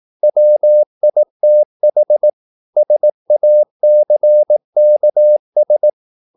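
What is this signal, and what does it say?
Morse code, heard as a keyed tone, about 18 wpm, 600 Hz. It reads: WITH SACKS
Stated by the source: Morse practice at 18 wpm, English text, from CW med Gnister.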